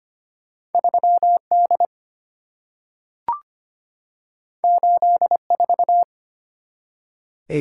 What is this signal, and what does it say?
Morse code, heard as a keyed tone, about 25 wpm, 690 Hz. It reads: 3D 84
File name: Two Letter-Number Combinations - 25wpm